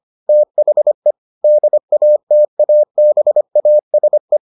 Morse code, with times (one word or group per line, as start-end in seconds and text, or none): 0.3-1.1 THE
1.4-4.4 DATABASE